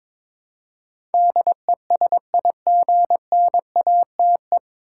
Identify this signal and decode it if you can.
Morse code, heard as a keyed tone, about 22 wpm, 700 Hz. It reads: DESIGNATE
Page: Morse Code Ninja practice 22wpm